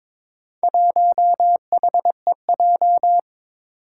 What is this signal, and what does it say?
Morse code, heard as a keyed tone, about 22 wpm, 700 Hz. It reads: 1HEJ
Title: Four Challenging Letter-Number Combinations 22wpm